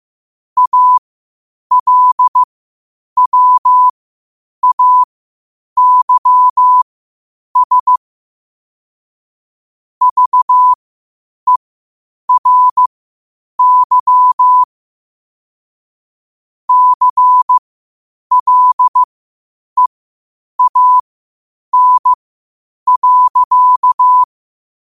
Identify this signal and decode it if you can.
Morse code, heard as a keyed tone, about 15 wpm, spaced as for 8 wpm, 1000 Hz. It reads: ALWAYS VERY CLEAN.
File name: qotdfiles / 8wpm